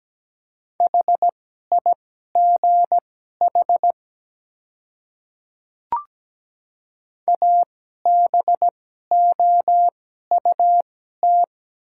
Morse code, heard as a keyed tone, about 17 wpm, spaced as for 12 wpm, 700 Hz. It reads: HIGH ABOUT
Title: 2x Character Spacing - Top 200 Words - 17wpm